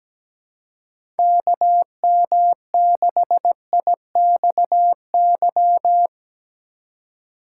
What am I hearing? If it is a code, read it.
Morse code, heard as a keyed tone, about 17 wpm, 700 Hz. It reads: KM6IXY
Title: USA Call Signs - 6 Characters - 4x Character Spacing - 17wpm